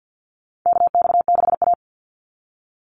Morse code, heard as a keyed tone, about 50 wpm, 700 Hz. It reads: KX6A
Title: NA SOTA Chaser Callsigns 50wpm